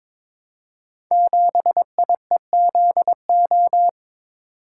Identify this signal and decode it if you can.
Morse code, heard as a keyed tone, about 22 wpm, 700 Hz. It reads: 7IEZO
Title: Code Groups - 5 Letter:Numbers - 22wpm